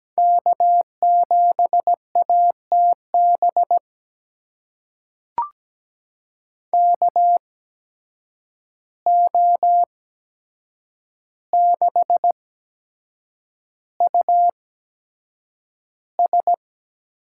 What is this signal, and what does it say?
Morse code, heard as a keyed tone, about 17 wpm, spaced as for 7 wpm, 700 Hz. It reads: K7ATB KO6US
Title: USA Call Signs - 5 Characters - 8x Character Spacing - 17wpm